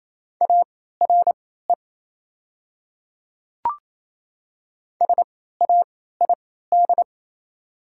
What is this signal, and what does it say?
Morse code, heard as a keyed tone, about 28 wpm, spaced as for 16 wpm, 700 Hz. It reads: ARE SAID
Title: 3x Character Spacing - Top 200 Words - 28wpm